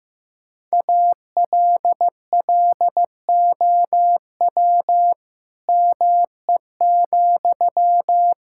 Morse code, heard as a keyed tone, about 15 wpm, 690 Hz. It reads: ALLOW ME,
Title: Sentences from Top 1000 Words - Review Part 5 - 15wpm